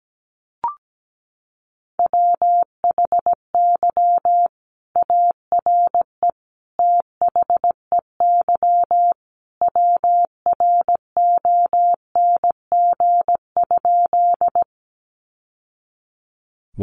Morse code, heard as a keyed tone, about 17 wpm, 700 Hz. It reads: WHY ARE THEY WRONG?
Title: Sentences from Top 900 Words 17wpm